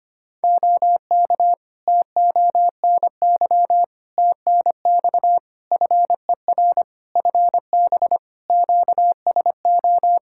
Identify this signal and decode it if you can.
Morse code, heard as a keyed tone, about 25 wpm, 700 Hz. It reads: OK TONY TNX FER FB QSO
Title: Practice QSO - Exchange 4 - Ending - 25wpm